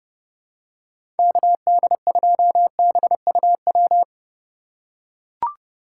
Morse code, heard as a keyed tone, about 32 wpm, 700 Hz. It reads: KD2BUW
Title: USA Call Signs - 6 Characters - 3x Character Spacing - 30wpm